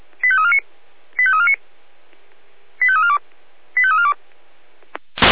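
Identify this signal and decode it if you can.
Five-tone selective call (ZVEI), heard as quick runs of five tones, tones of about 70 ms. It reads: 85349 85349 85342 85342
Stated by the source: ZVEI five-tone call 85886 received with an SDR.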